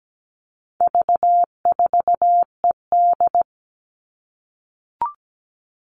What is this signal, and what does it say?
Morse code, heard as a keyed tone, about 17 wpm, 700 Hz. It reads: V4ED
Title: All Cut Numbers - 4-Digit 17wpm